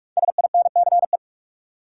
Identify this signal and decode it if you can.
Morse code, heard as a keyed tone, about 46 wpm, 700 Hz. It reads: SINCE
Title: Sets of 2 Words - New Word Drill - 45wpm